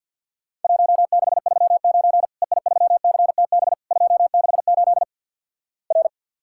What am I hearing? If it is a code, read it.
Morse code, heard as a keyed tone, about 50 wpm, 700 Hz. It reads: 1639 EI3CTB 268 R
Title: Worked All Europe DX Contest - QTC Exchange - 50wpm